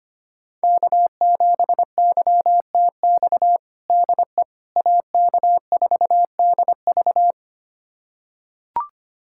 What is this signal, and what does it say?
Morse code, heard as a keyed tone, about 25 wpm, 700 Hz. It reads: K7YTX DE AK4DV
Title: USA Callsign DE Callsign - 5 Characters - 25wpm